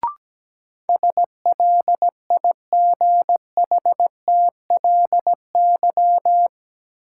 Morse code, heard as a keyed tone, about 17 wpm, 700 Hz. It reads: SLIGHTLY
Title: Sentences from Top 1300 Words - New Word Drill - 17wpm